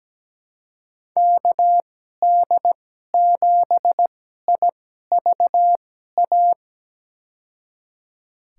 Morse code, heard as a keyed tone, about 17 wpm, spaced as for 12 wpm, 700 Hz. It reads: KD7IVA
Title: USA Call Signs - 2x Character Spacing - 17wpm